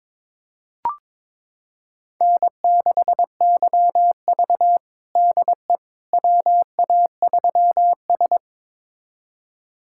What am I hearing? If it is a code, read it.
Morse code, heard as a keyed tone, about 22 wpm, 700 Hz. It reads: N6YV DE WA3S
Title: USA Callsign DE Callsign - 4 Characters - 22wpm